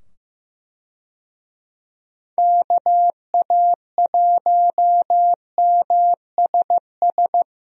Morse code, heard as a keyed tone, about 15 wpm, 700 Hz. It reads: KA1MSS
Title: USA Call Signs - 6 Characters - 15wpm